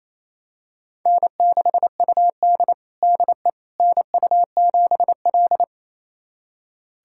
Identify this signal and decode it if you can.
Morse code, heard as a keyed tone, about 29 wpm, 700 Hz. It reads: N6UD DE NU7L
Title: USA Callsign DE Callsign - 4 Characters - 28wpm